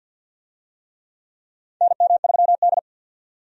Morse code, heard as a keyed tone, about 51 wpm, 700 Hz. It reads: NM3D
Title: USA Call Signs - 4 Characters - 50wpm